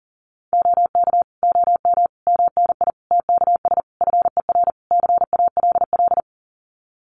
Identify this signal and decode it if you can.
Morse code, heard as a keyed tone, about 38 wpm, 700 Hz. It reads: OK OM MNI TKS FER CALL
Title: Common QSO Phrases 40wpm